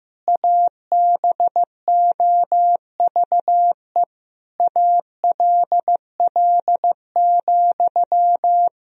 Morse code, heard as a keyed tone, about 16 wpm, 700 Hz. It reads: ABOVE ALL,
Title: Sentences from Top 1000 Words - Review Part 2 - 15wpm